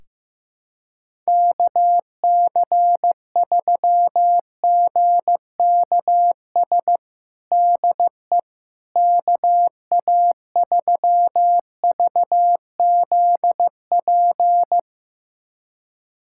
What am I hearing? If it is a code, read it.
Morse code, heard as a keyed tone, about 15 wpm, 700 Hz. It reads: KC3GKS DE KA3VZP
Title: USA Callsign DE Callsign - 6 Characters - 15wpm